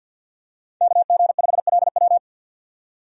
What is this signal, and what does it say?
Morse code, heard as a keyed tone, about 48 wpm, 700 Hz. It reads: KG5LW
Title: USA Call Signs - 5 Characters - 50wpm